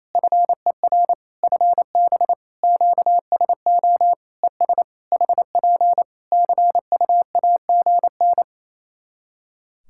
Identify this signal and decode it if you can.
Morse code, heard as a keyed tone, about 27 wpm, 700 Hz. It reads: FER FB QSO ES HP CUAGN
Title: K1USN SST QSO Practice 28wpm